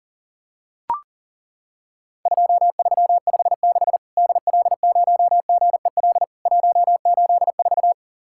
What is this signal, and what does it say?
Morse code, heard as a keyed tone, about 40 wpm, 700 Hz. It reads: 2356 DL0GEL 184
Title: Worked All Europe DX Contest - QTC Exchange - 40wpm